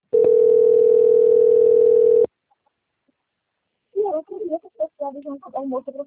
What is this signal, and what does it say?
Ringback tone (North America), 440 + 480 Hz, with speech.
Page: Braganca - SP